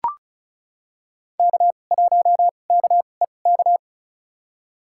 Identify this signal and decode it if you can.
Morse code, heard as a keyed tone, about 36 wpm, spaced as for 25 wpm, 700 Hz. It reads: K1KEK
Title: USA Call Signs - 5 Characters - 2x Character Spacing - 35wpm